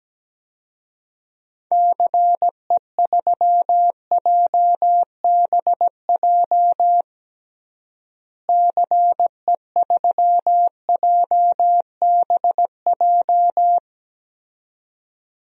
Morse code, heard as a keyed tone, about 17 wpm, 700 Hz. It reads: CE3JBJ CE3JBJ
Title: DXCC-100 Call Signs - 6 Characters - Repeated 2x - 17wpm